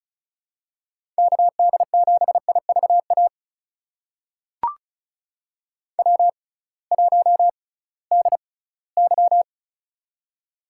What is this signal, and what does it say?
Morse code, heard as a keyed tone, about 36 wpm, spaced as for 20 wpm, 700 Hz. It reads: KD7IVA W1DY
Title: USA Call Signs - 6x Character Spacing - 35wpm